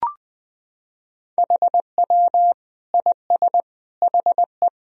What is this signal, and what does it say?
Morse code, heard as a keyed tone, about 20 wpm, 700 Hz. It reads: HW IS HE
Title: Sentences from Top 100 Words and Abbreviations 20wpm